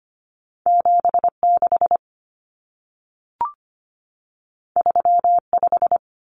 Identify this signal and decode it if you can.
Morse code, heard as a keyed tone, about 25 wpm, 700 Hz. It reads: 76 35
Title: Common Cut Numbers - 2-Digit 25wpm